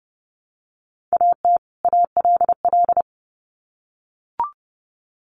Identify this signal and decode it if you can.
Morse code, heard as a keyed tone, about 30 wpm, 700 Hz. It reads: AT ALL
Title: Sets of 2 Words 30wpm